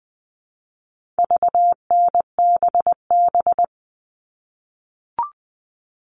Morse code, heard as a keyed tone, about 20 wpm, 700 Hz. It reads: VNBB
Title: All Cut Numbers - 4-Digit 20wpm